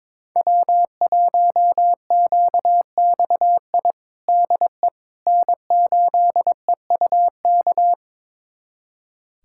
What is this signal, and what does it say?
Morse code, heard as a keyed tone, about 22 wpm, 700 Hz. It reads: W1QXI DE N8EUK
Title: USA Callsign DE Callsign - 5 Characters - 22wpm